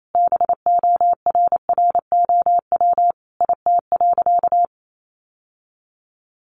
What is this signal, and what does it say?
Morse code, heard as a keyed tone, about 28 wpm, 700 Hz. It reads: BORROW IT.
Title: Sentences from Top 1500 Words 28wpm